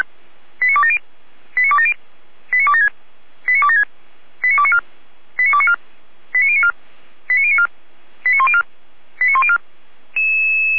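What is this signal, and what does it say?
Five-tone selective call (ZVEI), heard as quick runs of five tones, tones of about 70 ms. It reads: 89270 89270 89276 89276 89294 89294 89094 89094 89104 89104
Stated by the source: ZVEI five-tone selective call received by SDR, from the feuerwehr-orlamuende.org fire brigade audio server.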